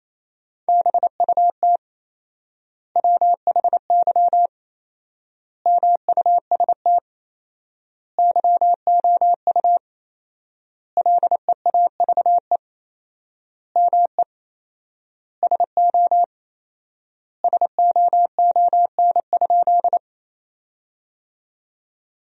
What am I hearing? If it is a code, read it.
Morse code, heard as a keyed tone, about 28 wpm, 700 Hz. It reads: BUT WHY MUST YOU LEAVE ME SO SOON?
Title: 4x Word-Spacing - Sentences from Top 400 Words - 28wpm